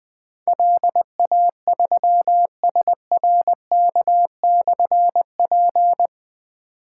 Morse code, heard as a keyed tone, about 20 wpm, 700 Hz. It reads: LA3SRK/P
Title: Difficult Call Signs 20wpm